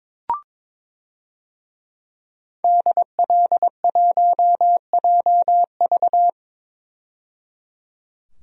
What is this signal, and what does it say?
Morse code, heard as a keyed tone, about 22 wpm, 700 Hz. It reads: DL1JV
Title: DXCC-100 Call Signs - 5 Characters - 22wpm